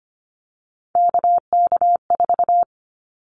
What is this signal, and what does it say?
Morse code, heard as a keyed tone, about 25 wpm, 700 Hz. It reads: KK4
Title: Callsign Prefixes for Top 25 DXCC - Repeated 3x - 25wpm